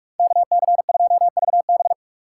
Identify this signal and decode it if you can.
Morse code, heard as a keyed tone, about 64 wpm, spaced as for 54 wpm, 700 Hz. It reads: KC2VB
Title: Speed Racing - USA 2x2 Call Signs 45wpm